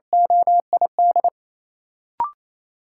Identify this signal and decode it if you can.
Morse code, heard as a keyed tone, about 28 wpm, 700 Hz. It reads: OID
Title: Sentences from Top 1200 Words - New Word Drill Alt - 28wpm